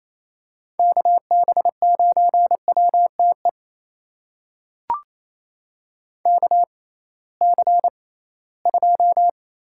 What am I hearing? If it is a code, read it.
Morse code, heard as a keyed tone, about 28 wpm, spaced as for 18 wpm, 700 Hz. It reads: KB9WTE KC2
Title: USA Call Signs - 6 Characters - 6x Character Spacing - 28wpm